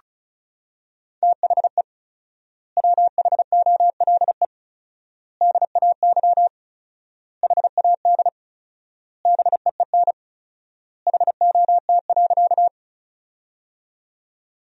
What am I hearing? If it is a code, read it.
Morse code, heard as a keyed tone, about 36 wpm, 700 Hz. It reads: THE WHOLE DAY HAD BEEN HOT.
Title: 4x Word-Spacing - Sentences from Top 500 Words - 35wpm